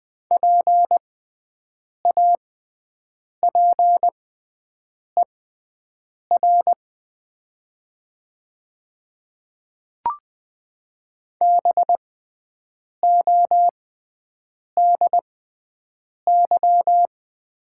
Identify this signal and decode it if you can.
Morse code, heard as a keyed tone, about 20 wpm, spaced as for 7 wpm, 700 Hz. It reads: PAPER BODY
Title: 6x Character Spacing - Top 500 Words - 20wpm